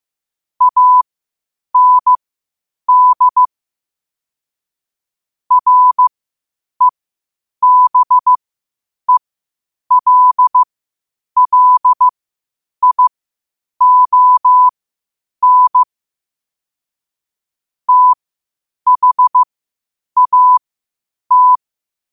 Morse code, heard as a keyed tone, about 15 wpm, spaced as for 8 wpm, 1000 Hz. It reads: AND REBELLION THAT
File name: qotdfiles / 8wpm